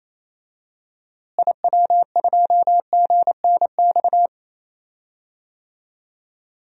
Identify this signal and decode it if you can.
Morse code, heard as a keyed tone, about 27 wpm, 700 Hz. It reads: IW2GNX
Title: DXCC-100 Call Signs - 6 Characters - Repeated 3x - 28wpm